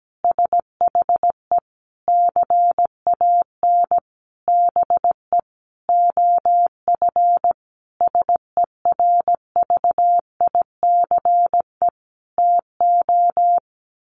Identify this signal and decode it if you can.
Morse code, heard as a keyed tone, about 17 wpm, 700 Hz. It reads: SHE CAN BE OF SERVICE TO
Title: Sentences from Top 1200 Words 17wpm